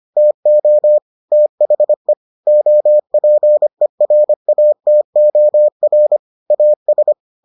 Morse code, heard as a keyed tone, about 25 wpm, 600 Hz. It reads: TO THE OPERATOR AS